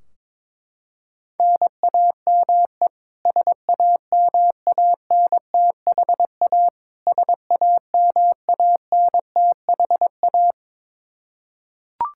Morse code, heard as a keyed tone, about 21 wpm, 700 Hz. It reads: NAME SAMANTHA SAMANTHA
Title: Practice QSO - Exchange 1 - Beginning - 22wpm